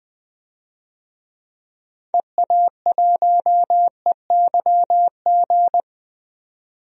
Morse code, heard as a keyed tone, about 20 wpm, 700 Hz. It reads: EA1EYG
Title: DXCC-100 Call Signs - 6 Characters - 20wpm